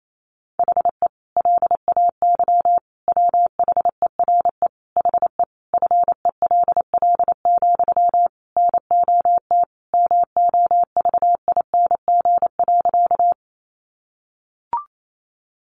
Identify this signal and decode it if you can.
Morse code, heard as a keyed tone, about 28 wpm, 700 Hz. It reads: HE LAY WHERE HE FELL, NOT MOVING.
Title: A Sound Of Thunder 28wpm